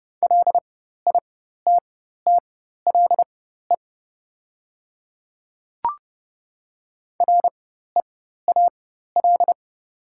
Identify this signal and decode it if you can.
Morse code, heard as a keyed tone, about 30 wpm, spaced as for 14 wpm, 690 Hz. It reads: LITTLE REAL
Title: 4x Character Spacing - Top 300 Words - 30wpm